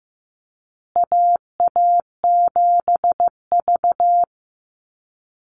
Morse code, heard as a keyed tone, about 15 wpm, 700 Hz. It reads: AA7V
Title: Call Signs Repeated 2x - 15wpm